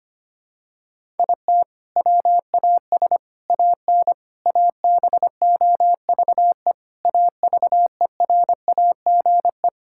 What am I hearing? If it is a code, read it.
Morse code, heard as a keyed tone, about 25 wpm, 700 Hz. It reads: IT WAS AN ABOVE AVERAGE
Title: Sentences from Top 1000 Words - Review Part 6 - 25wpm